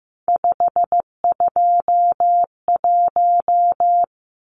Morse code, heard as a keyed tone, about 15 wpm, 700 Hz. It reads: 521
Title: Numbers - 3-Digit 15wpm